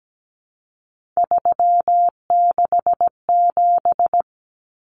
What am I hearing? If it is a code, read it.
Morse code, heard as a keyed tone, about 17 wpm, 700 Hz. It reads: 367
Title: Common Cut Numbers - 3-Digit 17wpm